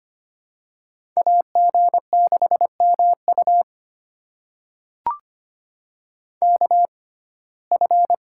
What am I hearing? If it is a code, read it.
Morse code, heard as a keyed tone, about 25 wpm, spaced as for 18 wpm, 700 Hz. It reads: AG6MU KF
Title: USA Call Signs - 6x Character Spacing - 25wpm